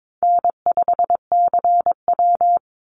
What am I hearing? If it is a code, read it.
Morse code, heard as a keyed tone, about 22 wpm, 690 Hz. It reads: N5CW
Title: Call Signs Repeated 3x - 22wpm